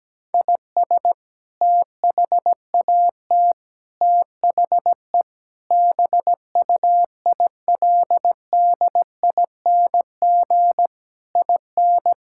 Morse code, heard as a keyed tone, about 17 wpm, 700 Hz. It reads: IS THAT THE BUILDING IN
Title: Sentences from Top 1000 Words - Review Part 5 - 17wpm